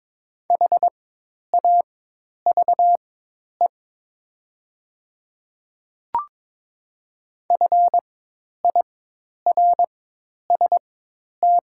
Morse code, heard as a keyed tone, about 22 wpm, spaced as for 10 wpm, 700 Hz. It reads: HAVE FIRST
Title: 4x Character Spacing - Top 200 Words - 22wpm